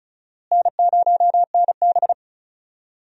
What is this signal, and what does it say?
Morse code, heard as a keyed tone, about 36 wpm, 700 Hz. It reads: N0NB
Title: USA Call Signs - 4 Characters - Repeated 3x - 35wpm